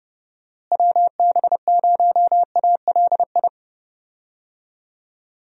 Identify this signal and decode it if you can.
Morse code, heard as a keyed tone, about 30 wpm, 700 Hz. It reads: WB0ALI
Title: USA Call Signs - 6 Characters - Repeated 2x - 30wpm